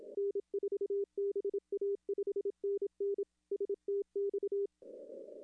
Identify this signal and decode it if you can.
Morse code, heard as a keyed tone, about 27 wpm, 410 Hz. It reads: N4BA 5NN STX
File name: contests / arrl-160-k5tr-op-w5tn -0300